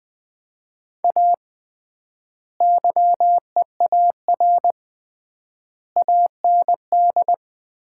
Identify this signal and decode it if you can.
Morse code, heard as a keyed tone, about 21 wpm, 700 Hz. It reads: A YEAR AND
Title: 3x Word-Spacing - Sets of 5 Words - 20wpm